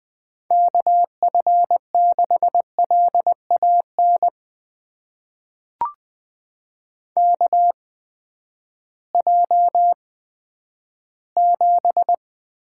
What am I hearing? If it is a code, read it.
Morse code, heard as a keyed tone, about 20 wpm, spaced as for 11 wpm, 700 Hz. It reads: KF6LAN KJ7
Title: USA Call Signs - 6 Characters - 8x Character Spacing - 20wpm